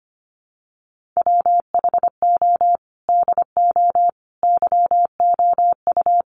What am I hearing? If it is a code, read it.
Morse code, heard as a keyed tone, about 25 wpm, 700 Hz. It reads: WHO DO YOU
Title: Sentences from Top 1500 Words 25wpm